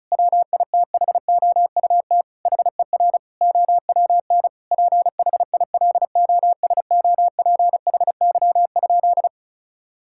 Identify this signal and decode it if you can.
Morse code, heard as a keyed tone, about 36 wpm, 700 Hz. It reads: WITHOUT HER OWN PHILOSOPHY?
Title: Sentences from Top 1000 Words - Review Part 6 - 35wpm